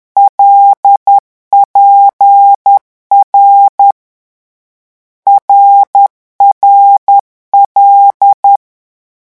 Morse code, heard as a keyed tone, about 10 wpm, 800 Hz. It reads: LPR RRL
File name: morse mp3 / lecon6-vitesse 10